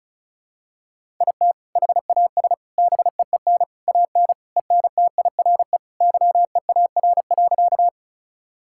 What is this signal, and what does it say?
Morse code, heard as a keyed tone, about 34 wpm, 700 Hz. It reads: IT HAS BEEN AN ENTIRE YEAR.
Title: Sentences from Top 1000 Words - Review Part 5 - 35wpm